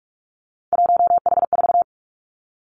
Morse code, heard as a keyed tone, about 46 wpm, 700 Hz. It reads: 1H4
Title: Three Challenging Letter-Number Combinations 45wpm